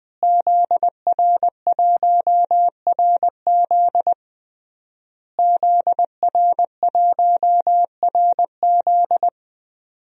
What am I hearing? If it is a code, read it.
Morse code, heard as a keyed tone, about 20 wpm, 700 Hz. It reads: ZR1RZ ZR1RZ
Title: DXCC-100 Call Signs - 5 Characters - Repeated 3x - 20wpm